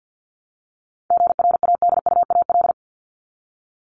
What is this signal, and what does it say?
Morse code, heard as a keyed tone, about 48 wpm, 700 Hz. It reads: GRADUAL
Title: LY Suffix - Top 100 Words with Base Words 50wpm